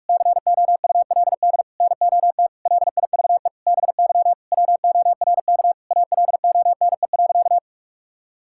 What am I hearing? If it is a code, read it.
Morse code, heard as a keyed tone, about 46 wpm, 700 Hz. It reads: COULD NOT LIVE BY WORK ALONE.
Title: Sentences from Top 1600 Words 45wpm